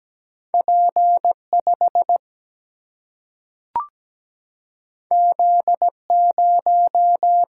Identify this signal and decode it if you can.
Morse code, heard as a keyed tone, about 17 wpm, 700 Hz. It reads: P 5 Z 0
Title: Single-Letter Single-Number 17wpm